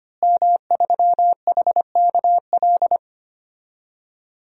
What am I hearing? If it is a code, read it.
Morse code, heard as a keyed tone, about 25 wpm, 700 Hz. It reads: M3HKL
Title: DXCC-100 Call Signs - 5 Characters - Repeated 2x - 25wpm